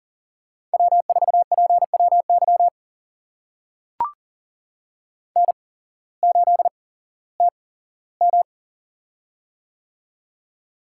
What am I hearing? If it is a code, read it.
Morse code, heard as a keyed tone, about 40 wpm, spaced as for 19 wpm, 700 Hz. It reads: W4PWY N8TM
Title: USA Call Signs - 8x Character Spacing - 40wpm